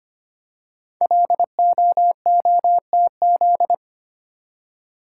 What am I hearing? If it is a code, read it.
Morse code, heard as a keyed tone, about 25 wpm, 700 Hz. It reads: LOOTZ
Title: Code Groups - 5 Letters - 25wpm